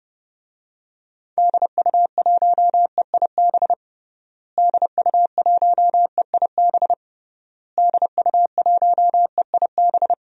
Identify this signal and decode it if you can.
Morse code, heard as a keyed tone, about 30 wpm, 700 Hz. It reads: DU1EIB DU1EIB DU1EIB
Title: DXCC-100 Call Signs - 6 Characters - Repeated 3x - 30wpm